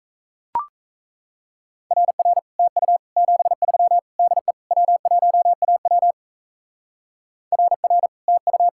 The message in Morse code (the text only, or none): RR TU 73 DE W1AW RR TU